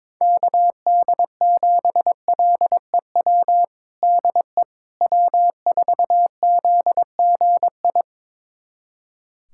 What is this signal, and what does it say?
Morse code, heard as a keyed tone, about 22 wpm, 700 Hz. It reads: KD7LEW DE W4ZGI
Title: USA Callsign DE Callsign 22wpm